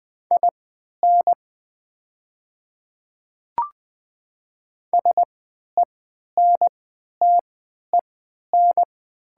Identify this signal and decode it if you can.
Morse code, heard as a keyed tone, about 20 wpm, spaced as for 11 wpm, 690 Hz. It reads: IN SENTEN
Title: 3x Character Spacing - Top 200 Words - 20wpm